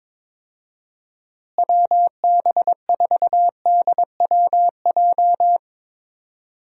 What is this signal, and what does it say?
Morse code, heard as a keyed tone, about 21 wpm, 700 Hz. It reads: WB4DWJ
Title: USA Call Signs - 6 Characters - Repeated 2x - 22wpm